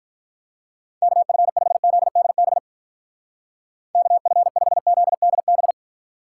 Morse code, heard as a keyed tone, about 45 wpm, 700 Hz. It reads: KF5ZDB KF5ZDB